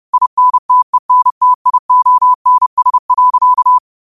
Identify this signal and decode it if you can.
Morse code, heard as a keyed tone, about 30 wpm, 1000 Hz. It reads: INTENTIONS.
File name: qotdfiles / 30wpm